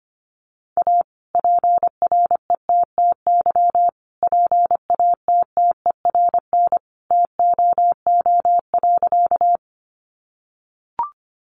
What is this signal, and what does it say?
Morse code, heard as a keyed tone, about 25 wpm, 700 Hz. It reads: A PRETTY PATTERN TOO.
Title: Sentences from Top 1000 Words 25wpm